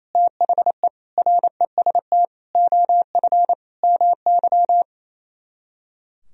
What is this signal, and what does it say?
Morse code, heard as a keyed tone, about 36 wpm, 700 Hz. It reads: THE REST OF MY